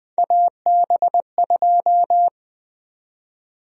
Morse code, heard as a keyed tone, about 20 wpm, 700 Hz. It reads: AB2
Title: USA Call Sign Prefixes - Repeated 2x - 20wpm